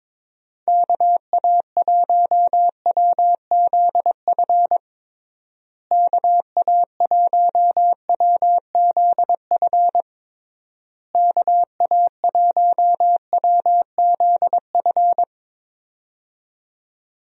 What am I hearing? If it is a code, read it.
Morse code, heard as a keyed tone, about 22 wpm, 700 Hz. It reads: KA1WZF KA1WZF KA1WZF
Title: USA Call Signs - 6 Characters - Repeated 3x - 22wpm